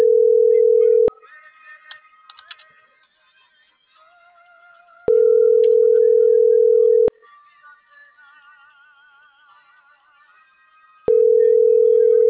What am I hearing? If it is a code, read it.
Ringback tone (North America), 440 + 480 Hz.